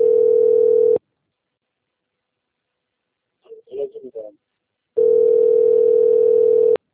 Ringback tone (North America), 440 + 480 Hz.